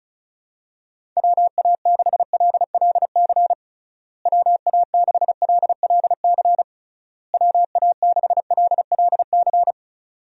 Morse code, heard as a keyed tone, about 36 wpm, 700 Hz. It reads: WA6LLC WA6LLC WA6LLC
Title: USA Call Signs - 6 Characters - Repeated 3x - 35wpm